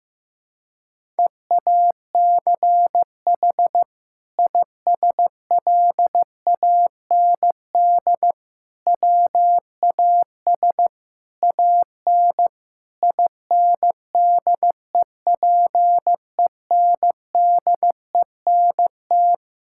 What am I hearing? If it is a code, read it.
Morse code, heard as a keyed tone, about 16 wpm, 700 Hz. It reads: EACH ISLAND WAS AN INDEPENDENT
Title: Sentences from Top 1000 Words - Review Part 6 - 15wpm